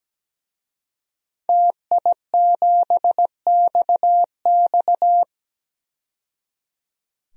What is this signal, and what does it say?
Morse code, heard as a keyed tone, about 17 wpm, 700 Hz. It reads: TI7XX
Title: DXCC-100 Call Signs - 5 Characters - 17wpm